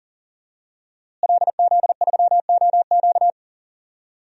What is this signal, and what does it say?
Morse code, heard as a keyed tone, about 40 wpm, 700 Hz. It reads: LZ3OQ
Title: DXCC-100 Call Signs - 5 Characters - Repeated 3x - 40wpm